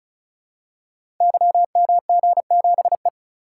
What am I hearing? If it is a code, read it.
Morse code, heard as a keyed tone, about 35 wpm, 690 Hz. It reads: YMG7E
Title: Code Groups - 5 Letter:Numbers - 35wpm